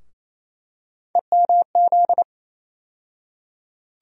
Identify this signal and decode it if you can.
Morse code, heard as a keyed tone, about 28 wpm, 700 Hz. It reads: EMZ
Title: USA Call Sign Postfixes - 3 Letters 28wpm